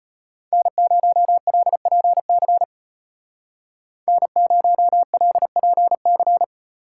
Morse code, heard as a keyed tone, about 33 wpm, 700 Hz. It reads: N0LPC N0LPC